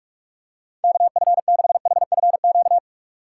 Morse code, heard as a keyed tone, about 42 wpm, 690 Hz. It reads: KF6HFQ
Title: USA Call Signs - 6 Characters - 3x Character Spacing - 45wpm